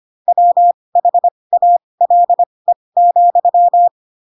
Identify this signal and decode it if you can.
Morse code, heard as a keyed tone, about 25 wpm, spaced as for 20 wpm, 700 Hz. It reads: WHALE,
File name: Pinocchio / Pinocchio ch6 15 25 ik1hge